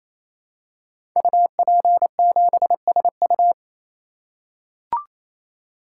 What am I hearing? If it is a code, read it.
Morse code, heard as a keyed tone, about 28 wpm, 700 Hz. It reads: UP7SU
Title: Code Groups - 5 Letter:Numbers - 28wpm